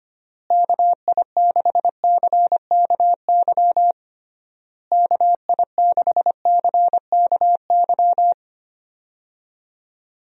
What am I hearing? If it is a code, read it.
Morse code, heard as a keyed tone, about 25 wpm, 700 Hz. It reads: KI6CKY KI6CKY
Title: USA Call Signs - 6 Characters - Repeated 3x - 25wpm